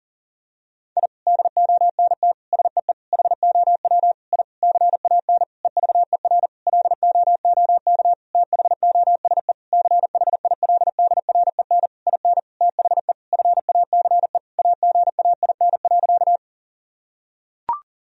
Morse code, heard as a keyed tone, about 40 wpm, 700 Hz. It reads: I DONT SEE HOW I CAN EVER LOOK THOSE CHILDREN IN THE FACE AGAIN.
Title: Sentences from Top 1000 Words - Review Part 2 - 40wpm